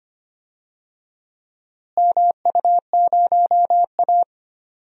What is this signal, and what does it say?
Morse code, heard as a keyed tone, about 25 wpm, 700 Hz. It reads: MU0A
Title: DXCC-100 Call Signs - 4 Characters - Repeated 2x - 25wpm